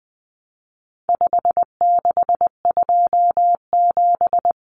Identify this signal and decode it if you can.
Morse code, heard as a keyed tone, about 20 wpm, 700 Hz. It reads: 5627
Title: Common Cut Numbers - 4-Digit 20wpm